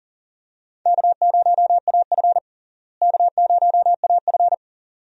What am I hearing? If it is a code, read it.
Morse code, heard as a keyed tone, about 39 wpm, 700 Hz. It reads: K0AF K0AF
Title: USA Call Signs - 4 Characters - Repeated 2x - 40wpm